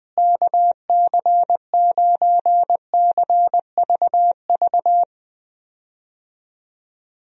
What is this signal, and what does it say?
Morse code, heard as a keyed tone, about 19 wpm, 700 Hz. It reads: KC9CVV
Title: USA Call Signs - 6 Characters - Repeated 3x - 20wpm